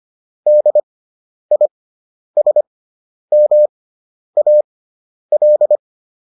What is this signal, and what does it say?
Morse code, heard as a keyed tone, about 25 wpm, spaced as for 10 wpm, 600 Hz. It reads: DISMAL